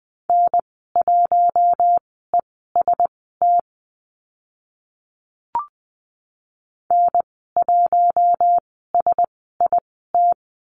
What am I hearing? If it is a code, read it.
Morse code, heard as a keyed tone, about 20 wpm, spaced as for 14 wpm, 700 Hz. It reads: N1EST N1SIT
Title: 042 Lesson 8g - Number 1 - Call Signs 20wpm